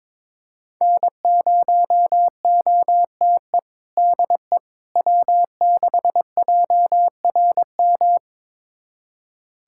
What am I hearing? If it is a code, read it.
Morse code, heard as a keyed tone, about 21 wpm, 700 Hz. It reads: N0OTE DE W6JRM
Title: USA Callsign DE Callsign - 5 Characters - 22wpm